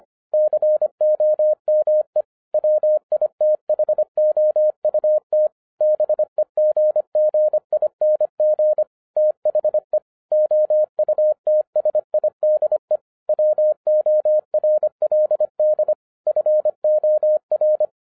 Morse code, heard as a keyed tone, about 26 wpm, 610 Hz. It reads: COME WITHOUT BEGGING THE OUTSIDE WORLD FOR
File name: 2012-01-mp3 / n14-FUN WITH NUMBERS-25wpm